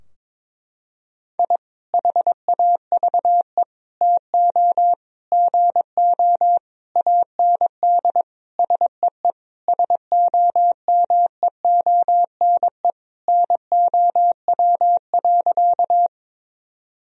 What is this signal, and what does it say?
Morse code, heard as a keyed tone, about 22 wpm, 700 Hz. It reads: I HAVE TO GO AND SEE SOMEONE NOW.
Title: Sentences from Top 1000 Words - Review Part 4 - 22wpm